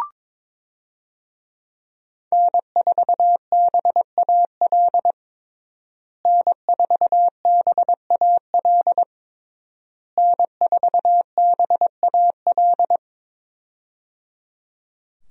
Morse code, heard as a keyed tone, about 22 wpm, 700 Hz. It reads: N4BAL N4BAL N4BAL
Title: USA Call Signs - 5 Characters - Repeated 3x - 22wpm